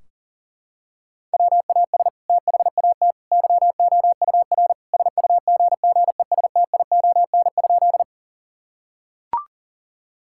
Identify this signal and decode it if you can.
Morse code, heard as a keyed tone, about 40 wpm, 700 Hz. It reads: WAS THAT YOUR SUGGESTION?
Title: Sentences from Top 1600 Words 40wpm